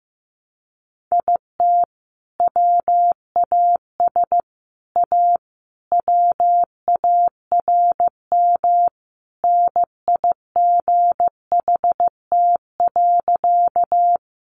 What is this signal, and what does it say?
Morse code, heard as a keyed tone, about 15 wpm, 700 Hz. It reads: IT WAS A WARM NIGHT.
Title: Sentences from Top 500 Words 15wpm